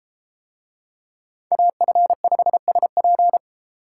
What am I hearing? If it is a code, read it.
Morse code, heard as a keyed tone, about 33 wpm, 700 Hz. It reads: AF5SP